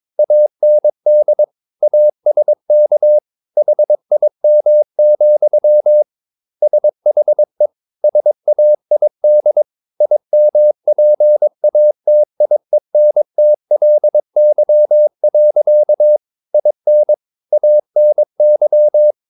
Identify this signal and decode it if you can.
Morse code, heard as a keyed tone, about 22 wpm, 600 Hz. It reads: AND ASK HIM, SHE SAID IMPATIENTLY. IN ANY
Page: Romanen Never af Ken Follet på engelsk.